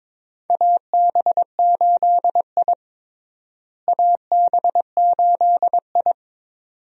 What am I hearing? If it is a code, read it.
Morse code, heard as a keyed tone, about 22 wpm, 700 Hz. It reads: AB8I AB8I
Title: USA Call Signs - 4 Characters - Repeated 2x - 22wpm